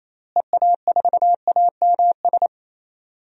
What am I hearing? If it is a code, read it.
Morse code, heard as a keyed tone, about 28 wpm, 700 Hz. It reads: EA4AMS
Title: DXCC-100 Call Signs - 6 Characters - Repeated 3x - 28wpm